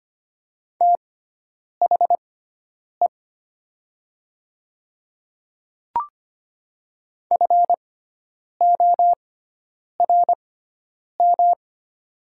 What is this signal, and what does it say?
Morse code, heard as a keyed tone, about 25 wpm, spaced as for 9 wpm, 700 Hz. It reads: THE FORM
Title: 6x Character Spacing - Top 200 Words - 25wpm